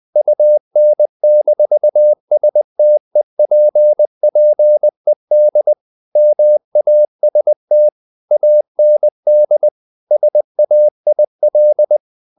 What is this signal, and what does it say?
Morse code, heard as a keyed tone, about 20 wpm, 600 Hz. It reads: UN-STEPPED MAST AND SAIL
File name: the old man and the sea / the old man and the sea part 1 20wpm